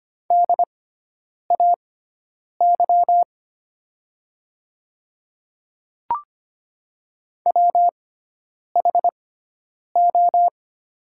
Morse code, heard as a keyed tone, about 25 wpm, spaced as for 9 wpm, 700 Hz. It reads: DAY WHO
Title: 6x Character Spacing - Top 200 Words - 25wpm